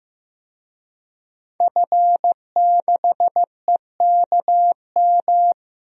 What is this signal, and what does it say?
Morse code, heard as a keyed tone, about 16 wpm, 700 Hz. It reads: F6EKM
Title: DXCC-100 Call Signs - 5 Characters - Repeated 2x - 15wpm